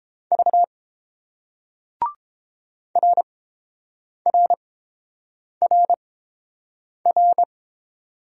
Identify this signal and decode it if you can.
Morse code, heard as a keyed tone, about 31 wpm, spaced as for 7 wpm, 700 Hz. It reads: V RRRR